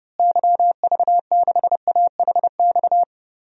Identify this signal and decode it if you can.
Morse code, heard as a keyed tone, about 30 wpm, 700 Hz. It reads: YV6AHX
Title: DXCC-100 Call Signs - 6 Characters - Repeated 3x - 30wpm